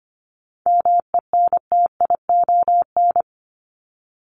Morse code, heard as a keyed tone, about 25 wpm, 700 Hz. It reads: MENTION
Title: ED Suffix - Top 100 Words with Base Words 25wpm